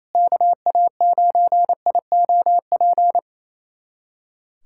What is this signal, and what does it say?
Morse code, heard as a keyed tone, about 28 wpm, 700 Hz. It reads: KA9IOP
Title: USA Callsign DE Callsign 28wpm